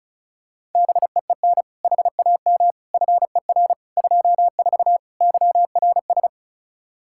Morse code, heard as a keyed tone, about 36 wpm, 700 Hz. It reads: BEEN HAM FER 24 YRS
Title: Practice QSO - Exchange 3 - More Info - 35wpm